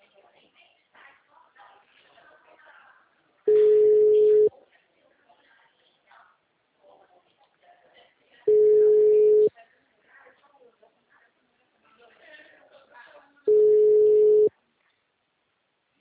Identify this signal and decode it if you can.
Ringback tone (Europe, 425 Hz), with speech.
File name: out- -2007- - - .65882